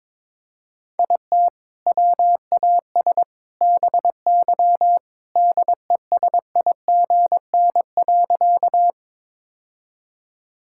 Morse code, heard as a keyed tone, about 22 wpm, 700 Hz. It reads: IT WAS BY DESIGN.
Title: Sentences from Top 1000 Words - Review Part 5 - 22wpm